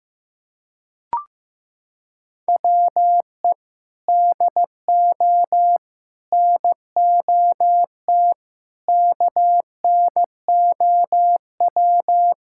Morse code, heard as a keyed tone, about 15 wpm, 700 Hz. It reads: WE DO NOT KNOW
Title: Sentences from Top 1000 Words - Review Part 6 - 15wpm